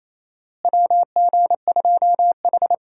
Morse code, heard as a keyed tone, about 27 wpm, 700 Hz. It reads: WG2H
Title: USA Call Signs - 4 Characters - Repeated 2x - 28wpm